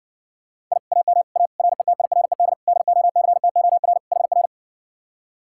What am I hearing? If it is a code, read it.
Morse code, heard as a keyed tone, about 59 wpm, 700 Hz. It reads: I AM A RETIRED DOCTOR HR